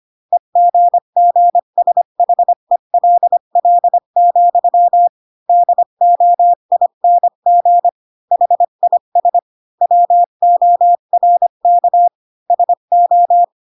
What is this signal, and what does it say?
Morse code, heard as a keyed tone, about 25 wpm, 700 Hz. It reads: EGGSHELL, DOING HIS WORK SO
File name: Pinocchio / Pinocchio ch8 20 25 ik1hge